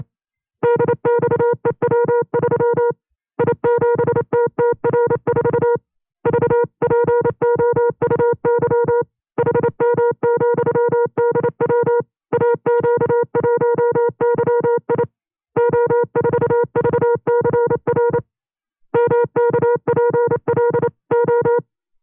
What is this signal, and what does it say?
Morse code, heard as a keyed tone, about 28 wpm, 480 Hz. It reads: DXEW3 I7TTR4 VPOUY HM,DW AQ1YI O4VCR MKPLO
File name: 25WPM